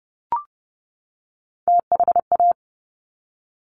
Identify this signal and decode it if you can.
Morse code, heard as a keyed tone, about 30 wpm, 700 Hz. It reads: THA
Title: Top Three Letter Combinations in Words 30wpm